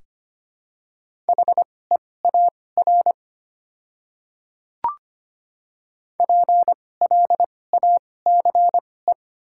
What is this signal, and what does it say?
Morse code, heard as a keyed tone, about 25 wpm, spaced as for 18 wpm, 690 Hz. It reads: HEAR PLACE
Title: 2x Character Spacing - Top 500 Words - 25wpm